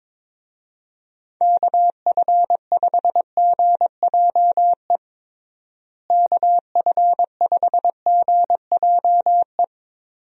Morse code, heard as a keyed tone, about 21 wpm, 700 Hz. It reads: KF5GJE KF5GJE
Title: USA Call Signs - 6 Characters - Repeated 3x - 22wpm